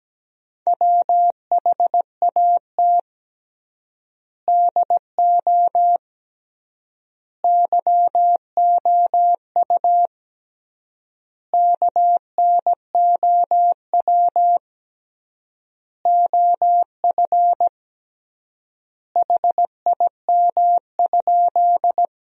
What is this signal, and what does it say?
Morse code, heard as a keyed tone, about 17 wpm, 700 Hz. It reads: WHAT DO YOU KNOW OF HIM?
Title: 3x Word-Spacing - Sentences from Top 100 Words - 17wpm